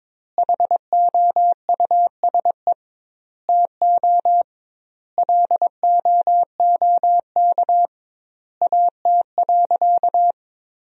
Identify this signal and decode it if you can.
Morse code, heard as a keyed tone, about 22 wpm, 700 Hz. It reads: HOUSE TO LOOK AT.
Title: 2x Word-Spacing - Sentences from Top 200 Words - 22wpm